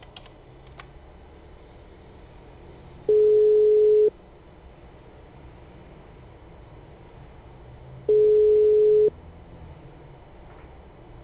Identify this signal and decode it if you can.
Ringback tone (Europe, 425 Hz).